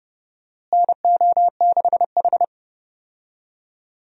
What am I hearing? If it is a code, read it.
Morse code, heard as a keyed tone, about 30 wpm, 700 Hz. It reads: NO6H
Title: USA Call Signs - 4 Characters - Repeated 2x - 30wpm